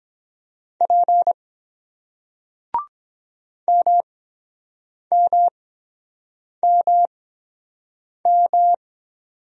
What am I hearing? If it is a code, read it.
Morse code, heard as a keyed tone, about 23 wpm, spaced as for 7 wpm, 700 Hz. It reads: P MMMM